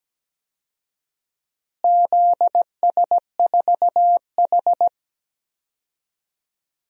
Morse code, heard as a keyed tone, about 17 wpm, 700 Hz. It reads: ZS4H
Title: DXCC-100 Call Signs - 4 Characters - Repeated 3x - 17wpm